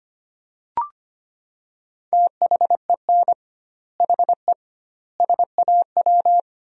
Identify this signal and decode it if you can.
Morse code, heard as a keyed tone, about 25 wpm, 700 Hz. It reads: THEN HE SAW
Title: 2x Word-Spacing - Sentences from Top 400 Words - 25wpm